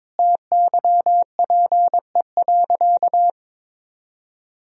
Morse code, heard as a keyed tone, about 22 wpm, 700 Hz. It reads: TYPE.
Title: Sentences from Top 1000 Words - Review Part 3 - 22wpm